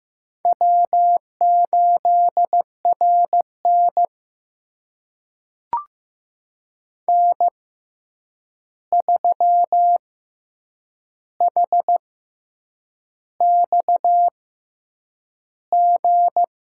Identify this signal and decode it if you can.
Morse code, heard as a keyed tone, about 15 wpm, spaced as for 7 wpm, 700 Hz. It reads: W8RN N3HXG
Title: USA Call Signs - 6x Character Spacing - 15wpm